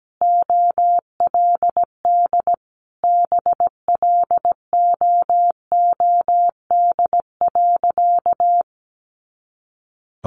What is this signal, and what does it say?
Morse code, heard as a keyed tone, about 17 wpm, 700 Hz. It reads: OLD BLOOD.
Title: A Sound Of Thunder 17wpm